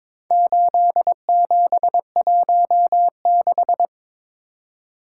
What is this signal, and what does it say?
Morse code, heard as a keyed tone, about 22 wpm, 700 Hz. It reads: 8716
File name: practice / 22wpm / Four Letter-Number Combinations - 22wpm